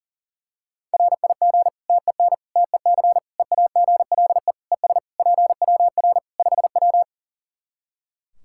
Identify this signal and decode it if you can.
Morse code, heard as a keyed tone, about 40 wpm, 700 Hz. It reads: RIG TEN TEC EAGLE ES PWR 5W
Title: Practice QSO - Exchange 2 - Info - 40wpm